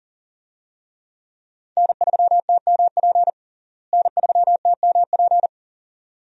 Morse code, heard as a keyed tone, about 40 wpm, 700 Hz. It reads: N3TMP N3TMP
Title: USA Call Signs - 5 Characters - Repeated 2x - 40wpm